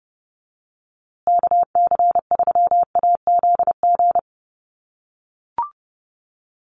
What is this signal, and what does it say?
Morse code, heard as a keyed tone, about 30 wpm, 700 Hz. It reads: KC3AZG
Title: USA Call Signs Repeated 2x 30wpm